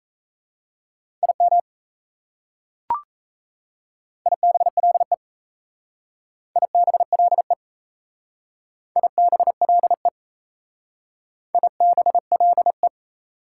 Morse code, heard as a keyed tone, about 34 wpm, 700 Hz. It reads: IM IBLE IBLE IBLE IBLE
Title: Speed-Racing Prefixes and Suffixes 28wpm